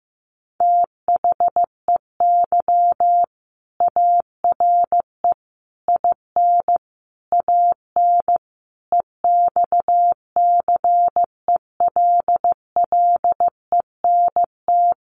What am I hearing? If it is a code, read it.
Morse code, heard as a keyed tone, about 15 wpm, 700 Hz. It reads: THEY ARE IN AN EXCELLENT
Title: Sentences from Top 1000 Words 15wpm